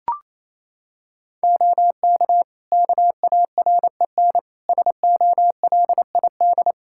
Morse code, heard as a keyed tone, about 28 wpm, 700 Hz. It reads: OK KAREN SOLID
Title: Practice QSO - Exchange 3 - More Info - 28wpm